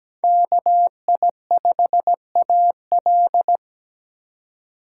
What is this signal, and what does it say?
Morse code, heard as a keyed tone, about 17 wpm, 700 Hz. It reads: KI5AL
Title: K1USN SST QSO Practice - 17wpm